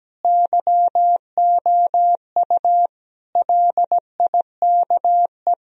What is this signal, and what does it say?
Morse code, heard as a keyed tone, about 17 wpm, 700 Hz. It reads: YOU LIKE
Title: Sentences from Top 1000 Words - Review Part 1 - 17wpm